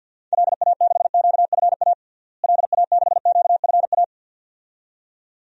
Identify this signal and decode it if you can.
Morse code, heard as a keyed tone, about 52 wpm, 700 Hz. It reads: LA6QFA LA6QFA
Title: DXCC-100 Call Signs - 6 Characters - Repeated 2x - 50wpm